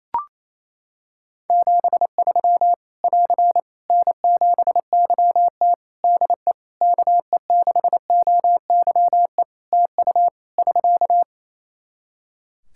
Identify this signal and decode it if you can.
Morse code, heard as a keyed tone, about 28 wpm, 700 Hz. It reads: 73 <AR> N7YT DE KE6OYE TU <SK>
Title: Practice QSO - Exchange 4 - Ending - 28wpm